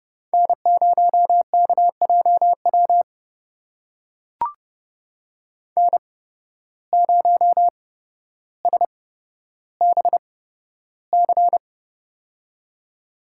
Morse code, heard as a keyed tone, about 31 wpm, spaced as for 13 wpm, 700 Hz. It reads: N0KJW N0SBC
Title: USA Call Signs - 5 Characters - 8x Character Spacing - 30wpm